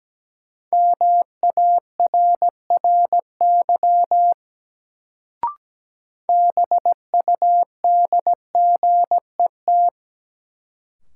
Morse code, heard as a keyed tone, about 17 wpm, 700 Hz. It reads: MARRY BUDGET
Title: Sentences from Top 1200 Words - New Word Drill Alt - 17wpm